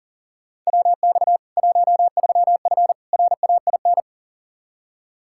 Morse code, heard as a keyed tone, about 40 wpm, 700 Hz. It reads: WX 13F RAIN
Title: Practice QSO - Exchange 2 - Info - 40wpm